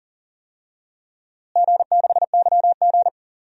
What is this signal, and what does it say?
Morse code, heard as a keyed tone, about 40 wpm, 700 Hz. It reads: G6YG
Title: DXCC-100 Call Signs - 4 Characters - Repeated 2x - 40wpm